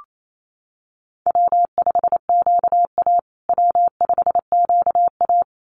Morse code, heard as a keyed tone, about 28 wpm, 700 Hz. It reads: W5QA W5QA
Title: USA Call Signs Repeated 2x 28wpm